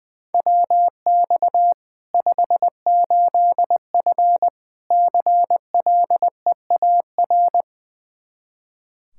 Morse code, heard as a keyed tone, about 21 wpm, 700 Hz. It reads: WX 58F CLEAR
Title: Practice QSO - Exchange 2 - Info - 20wpm